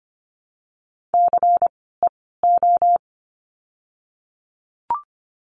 Morse code, heard as a keyed tone, about 25 wpm, spaced as for 16 wpm, 700 Hz. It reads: CEO
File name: course / 25:10wpm / 094 Lesson 16f - Letter c - Words 25wpm